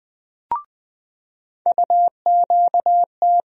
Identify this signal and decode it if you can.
Morse code, heard as a keyed tone, about 21 wpm, 700 Hz. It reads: UQT
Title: USA Call Sign Postfixes - 3 Letters 20wpm